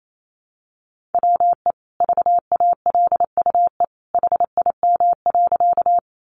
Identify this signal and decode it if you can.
Morse code, heard as a keyed tone, about 28 wpm, 700 Hz. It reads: WE VALUE HIM.
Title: Sentences from Top 600 Words 28wpm